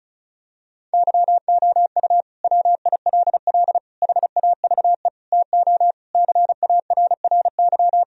Morse code, heard as a keyed tone, about 36 wpm, 700 Hz. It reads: YOU WILL HAVE TO CARRY
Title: Sentences from Top 1000 Words - Review Part 1 - 35wpm